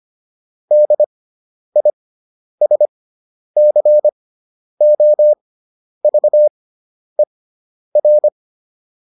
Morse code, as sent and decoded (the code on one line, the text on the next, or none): -.. .. ... -.-. --- ...- . .-.
DISCOVER